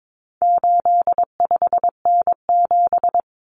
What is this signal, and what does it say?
Morse code, heard as a keyed tone, about 22 wpm, 700 Hz. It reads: 85N7
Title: Common Cut Numbers - 4-Digit 22wpm